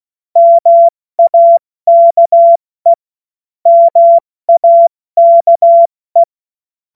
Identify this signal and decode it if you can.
Morse code, heard as a keyed tone, about 16 wpm, 680 Hz. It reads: MAKE MAKE
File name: Morsekurs / Witze / Witz 27. September 2022 70 F80